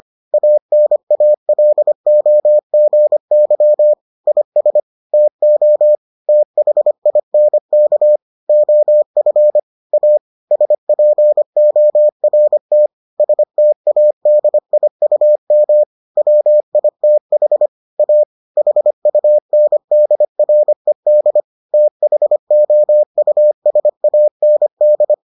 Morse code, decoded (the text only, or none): ANALOGY IS TO THINK OF A SPORT STADIUM WITH A HUNDRED THOUSAND